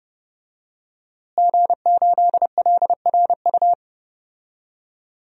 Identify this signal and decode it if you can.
Morse code, heard as a keyed tone, about 30 wpm, 700 Hz. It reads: G8LRU
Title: DXCC-100 Call Signs - 5 Characters - Repeated 3x - 30wpm